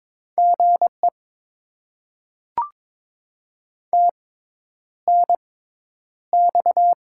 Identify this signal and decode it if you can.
Morse code, heard as a keyed tone, about 22 wpm, spaced as for 10 wpm, 700 Hz. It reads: GE TNX
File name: Top 20 QSO Elements - 6x Character Spacing - 22wpm